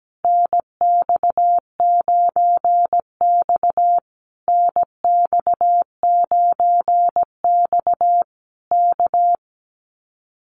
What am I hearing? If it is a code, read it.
Morse code, heard as a keyed tone, about 17 wpm, 700 Hz. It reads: NX9X NX9X K
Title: Common QSO Phrases Encore 17wpm